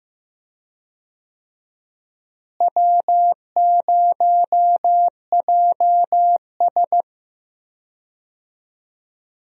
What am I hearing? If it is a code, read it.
Morse code, heard as a keyed tone, about 15 wpm, 700 Hz. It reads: W0JS
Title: USA Call Signs - 4 Characters - Repeated 3x - 15wpm